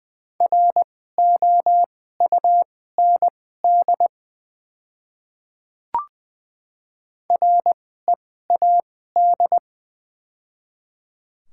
Morse code, heard as a keyed tone, about 21 wpm, spaced as for 15 wpm, 700 Hz. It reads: ROUND READ
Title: 2x Character Spacing - Top 200 Words - 20wpm